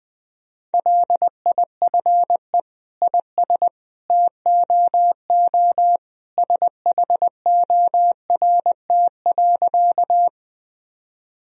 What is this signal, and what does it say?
Morse code, heard as a keyed tone, about 21 wpm, 700 Hz. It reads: LIFE IS TOO SHORT.
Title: Sentences from Top 1000 Words - Review Part 2 - 20wpm